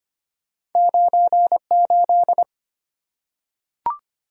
Morse code, heard as a keyed tone, about 25 wpm, 700 Hz. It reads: 98
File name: Two Letter-Number Combinations - 25wpm